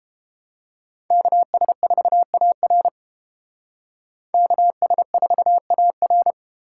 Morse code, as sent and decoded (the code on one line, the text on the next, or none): -.- ... ....- .- .-. / -.- ... ....- .- .-.
KS4AR KS4AR